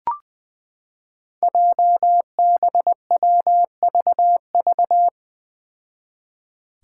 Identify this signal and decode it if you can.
Morse code, heard as a keyed tone, about 20 wpm, 700 Hz. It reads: JBWVV
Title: Five Challenging Letter-Number Combinations 20wpm